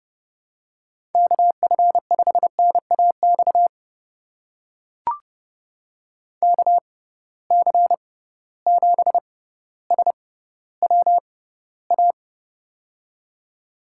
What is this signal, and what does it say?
Morse code, heard as a keyed tone, about 31 wpm, spaced as for 16 wpm, 700 Hz. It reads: KF5NAX KC7SWA
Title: USA Call Signs - 6 Characters - 6x Character Spacing - 30wpm